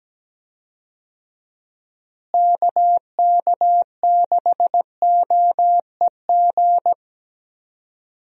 Morse code, heard as a keyed tone, about 17 wpm, 700 Hz. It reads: KK6OEG
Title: USA Call Signs - 6 Characters - Repeated 3x - 17wpm